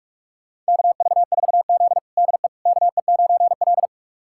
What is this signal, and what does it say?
Morse code, heard as a keyed tone, about 45 wpm, 700 Hz. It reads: KV4Z DE KE9L
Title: USA Callsign DE Callsign - 4 Characters - 45wpm